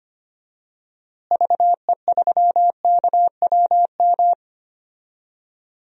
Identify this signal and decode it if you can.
Morse code, heard as a keyed tone, about 25 wpm, 700 Hz. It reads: VE3KWM
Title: DXCC-100 Call Signs - 6 Characters - Repeated 3x - 25wpm